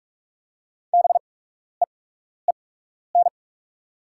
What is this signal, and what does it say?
Morse code, heard as a keyed tone, about 48 wpm, spaced as for 12 wpm, 690 Hz. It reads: BEEN
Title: 8x Character Spacing - Top 200 Words - 45wpm